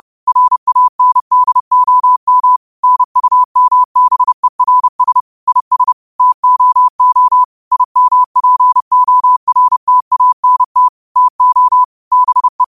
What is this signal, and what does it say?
Morse code, heard as a keyed tone, about 30 wpm, 1000 Hz. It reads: RANDOM NUMBERS IS TOO IMPORTANT TO BE
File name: qotdfiles / 30wpm